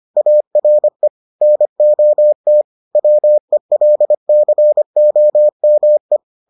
Morse code, heard as a keyed tone, about 25 wpm, 600 Hz. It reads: ARE NOT WELCOME